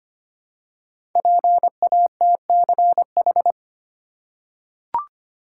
Morse code, heard as a keyed tone, about 25 wpm, 700 Hz. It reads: PATCH
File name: 5 Letter Words - First 3 Letters Given - 25wpm